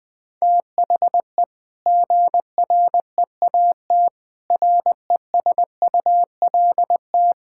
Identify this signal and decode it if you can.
Morse code, heard as a keyed tone, about 20 wpm, 700 Hz. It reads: THE GREAT RESULT
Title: Sentences from Top 1000 Words - Review Part 5 - 20wpm